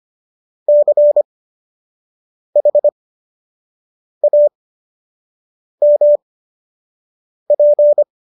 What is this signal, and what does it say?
Morse code, heard as a keyed tone, about 25 wpm, spaced as for 6 wpm, 590 Hz. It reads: CHAMP